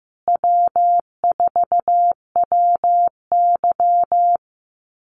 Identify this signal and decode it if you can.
Morse code, heard as a keyed tone, about 15 wpm, 700 Hz. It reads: W4WY
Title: USA Call Signs Repeated 3x 15wpm